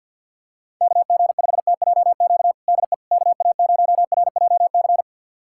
Morse code, heard as a keyed tone, about 50 wpm, 700 Hz. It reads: KG5TJQ DE KA0RJC
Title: USA Callsign DE Callsign - 6 Characters - 50wpm